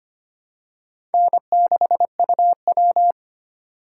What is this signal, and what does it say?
Morse code, heard as a keyed tone, about 25 wpm, 700 Hz. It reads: N6UW
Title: USA Call Signs - 4 Characters - 3x Character Spacing - 25wpm